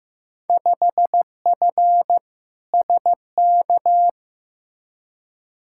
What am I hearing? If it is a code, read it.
Morse code, heard as a keyed tone, about 16 wpm, 700 Hz. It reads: 5F SK
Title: Field Day Exchanges Encore - 15wpm